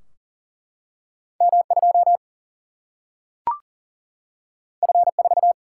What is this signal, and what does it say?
Morse code, heard as a keyed tone, about 39 wpm, 700 Hz. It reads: M2 F4
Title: Single-Letter Single-Number 40wpm